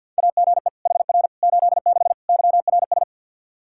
Morse code, heard as a keyed tone, about 51 wpm, 700 Hz. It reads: AGE HR 86 YRS
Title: Practice QSO - Exchange 3 - More Info - 50wpm